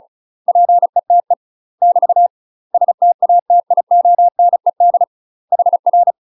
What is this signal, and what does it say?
Morse code, heard as a keyed tone, about 36 wpm, 700 Hz. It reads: PETE <BT> STATIONED HR
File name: UK MASTER QSOs TRAINING / 35WPM / UK MASTER QSOs 0044